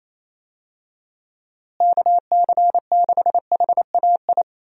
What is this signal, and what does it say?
Morse code, heard as a keyed tone, about 28 wpm, 700 Hz. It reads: KC6HAI
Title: USA Call Signs - 6 Characters - Repeated 3x - 28wpm